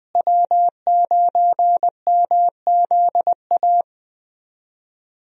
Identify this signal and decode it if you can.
Morse code, heard as a keyed tone, about 21 wpm, 700 Hz. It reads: W9MZA
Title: USA Call Signs - 5 Characters - Repeated 2x - 20wpm